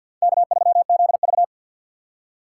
Morse code, heard as a keyed tone, about 53 wpm, 700 Hz. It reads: K3ZV
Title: USA Call Signs - 4 Characters - 2x Character Spacing - 50wpm